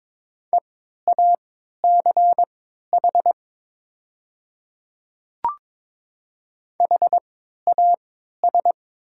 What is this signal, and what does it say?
Morse code, heard as a keyed tone, about 22 wpm, spaced as for 12 wpm, 700 Hz. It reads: EACH HAS
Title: 3x Character Spacing - Top 100 Words - 22wpm